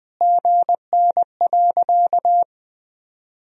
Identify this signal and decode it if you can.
Morse code, heard as a keyed tone, about 20 wpm, 700 Hz. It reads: GN.
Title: Sentences from Top 300 Words and Abbreviations 20wpm